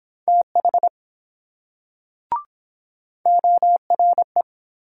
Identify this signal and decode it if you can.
Morse code, heard as a keyed tone, about 23 wpm, 690 Hz. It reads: TH ORE